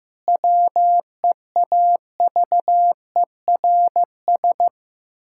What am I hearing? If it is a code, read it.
Morse code, heard as a keyed tone, about 15 wpm, 700 Hz. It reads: WEAVERS
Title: Pangram Challenge v2 - 15wpm